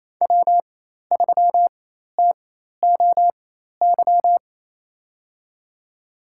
Morse code, heard as a keyed tone, about 28 wpm, spaced as for 13 wpm, 700 Hz. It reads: W3TOY
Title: USA Call Signs - 4x Character Spacing - 28wpm